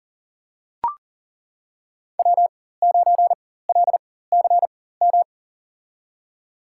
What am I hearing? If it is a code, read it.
Morse code, heard as a keyed tone, about 40 wpm, spaced as for 19 wpm, 700 Hz. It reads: W9LCM
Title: USA Call Signs - 4x Character Spacing - 40wpm